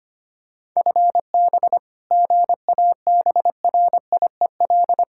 Morse code, heard as a keyed tone, about 25 wpm, 700 Hz. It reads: FB GABRIEL
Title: Practice QSO - Exchange 3 - More Info - 25wpm